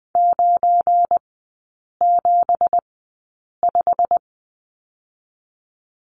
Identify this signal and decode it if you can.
Morse code, heard as a keyed tone, about 20 wpm, 700 Hz. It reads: 9 7 5
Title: 172 Lesson 28e - Number 9 - Three Characters 20wpm